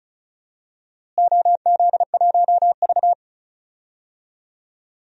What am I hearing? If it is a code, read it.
Morse code, heard as a keyed tone, about 34 wpm, 700 Hz. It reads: OZ1V
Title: DXCC-100 Call Signs - 4 Characters - Repeated 2x - 35wpm